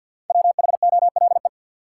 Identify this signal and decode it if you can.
Morse code, heard as a keyed tone, about 53 wpm, 700 Hz. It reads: WHOLE
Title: Five Letter Words - 50wpm